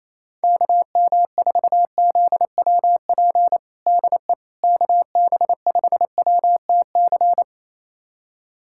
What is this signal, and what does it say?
Morse code, heard as a keyed tone, about 27 wpm, 700 Hz. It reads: KM4ZWP DE KB5WTC
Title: USA Callsign DE Callsign 28wpm